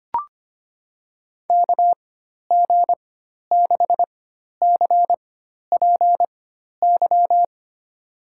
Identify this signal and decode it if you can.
Morse code, heard as a keyed tone, about 25 wpm, spaced as for 12 wpm, 700 Hz. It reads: KG6CPY
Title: USA Call Signs - 6 Characters - 4x Character Spacing - 25wpm